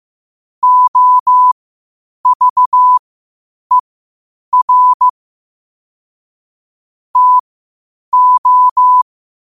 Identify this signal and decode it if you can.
Morse code, heard as a keyed tone, about 15 wpm, spaced as for 8 wpm, 1000 Hz. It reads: OVER TO
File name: qotdfiles / 8wpm